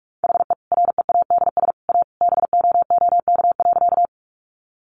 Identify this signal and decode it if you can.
Morse code, heard as a keyed tone, about 44 wpm, 700 Hz. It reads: HE READS A BOOK.
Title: Additional Easy Phrases 45wpm